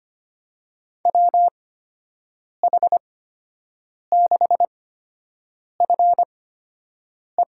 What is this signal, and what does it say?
Morse code, heard as a keyed tone, about 25 wpm, spaced as for 7 wpm, 700 Hz. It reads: WH6FE
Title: USA Call Signs - 6 Characters - 8x Character Spacing - 25wpm